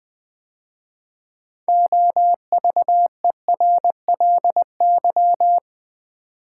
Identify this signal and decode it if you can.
Morse code, heard as a keyed tone, about 20 wpm, 700 Hz. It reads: OVERLY
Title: 6 Letter Words - First 3 Letters Given - 20wpm